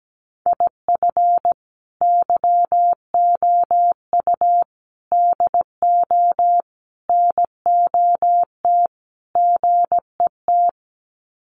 Morse code, heard as a keyed tone, about 17 wpm, 700 Hz. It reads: IF YOU DO NOT GET
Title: Sets of 5 Words Encore2 17wpm